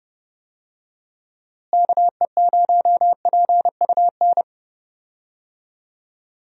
Morse code, heard as a keyed tone, about 30 wpm, 700 Hz. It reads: KE0PUN
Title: USA Call Signs - 6 Characters - Repeated 2x - 30wpm